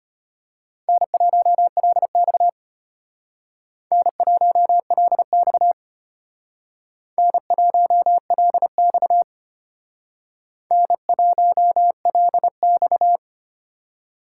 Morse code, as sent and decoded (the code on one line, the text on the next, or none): -. .---- .-.. -..- / -. .---- .-.. -..- / -. .---- .-.. -..- / -. .---- .-.. -..-
N1LX N1LX N1LX N1LX